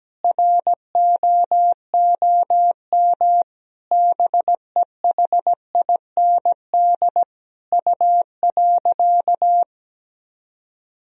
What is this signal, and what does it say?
Morse code, heard as a keyed tone, about 17 wpm, 700 Hz. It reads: ROOM BEHIND U.
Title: Sentences from Top 500 Words and Abbreviations 17wpm